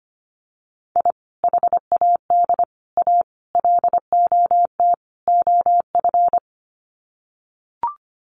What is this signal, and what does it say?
Morse code, heard as a keyed tone, about 25 wpm, 700 Hz. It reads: I HAD A LOT OF
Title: Sets of 5 Words Encore 25wpm